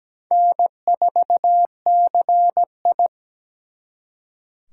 Morse code, heard as a keyed tone, about 20 wpm, 700 Hz. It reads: N4CI
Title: Speed Racing - USA 1x2 Call Signs 17wpm